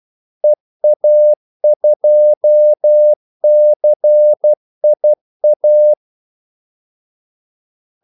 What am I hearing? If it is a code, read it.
Morse code, heard as a keyed tone, about 12 wpm, 600 Hz. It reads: EA2CIA
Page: 2 ekstra ordmellemrum mellem kaldesignalerne